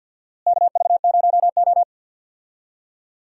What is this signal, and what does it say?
Morse code, heard as a keyed tone, about 47 wpm, 700 Hz. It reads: KV0O
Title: USA Call Signs - 4 Characters - Repeated 2x - 50wpm